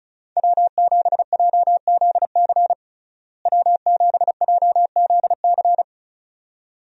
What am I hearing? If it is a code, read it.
Morse code, heard as a keyed tone, about 34 wpm, 700 Hz. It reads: W7JZC W7JZC
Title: USA Call Signs - 5 Characters - Repeated 2x - 35wpm